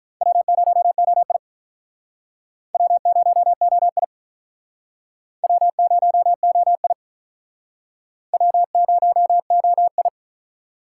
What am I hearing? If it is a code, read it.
Morse code, heard as a keyed tone, about 49 wpm, 700 Hz. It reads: W0OI W0OI W0OI W0OI